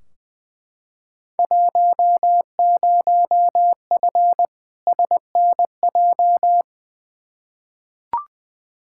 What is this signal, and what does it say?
Morse code, heard as a keyed tone, about 20 wpm, 700 Hz. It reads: 10F SNJ
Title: Field Day Exchanges Encore - 20wpm